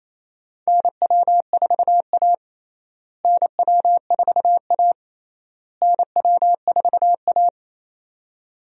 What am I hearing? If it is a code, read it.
Morse code, heard as a keyed tone, about 28 wpm, 700 Hz. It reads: NW4A NW4A NW4A
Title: USA Call Signs - 4 Characters - Repeated 3x - 28wpm